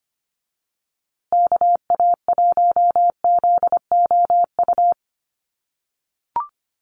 Morse code, heard as a keyed tone, about 25 wpm, 700 Hz. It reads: KA1ZOU
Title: USA Call Signs Repeated 2x 25wpm